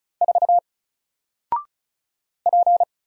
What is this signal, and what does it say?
Morse code, heard as a keyed tone, about 33 wpm, 700 Hz. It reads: 4 P
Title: Single Letter-Number 35wpm